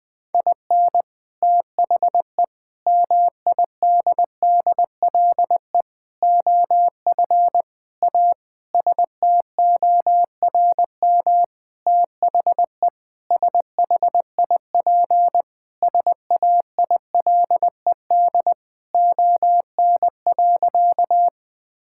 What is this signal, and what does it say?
Morse code, heard as a keyed tone, about 19 wpm, 700 Hz. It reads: IN THE MIDDLE OF A STORM THE SHIP SAILED ON.
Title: Sentences with 5-Word Phrases - 20wpm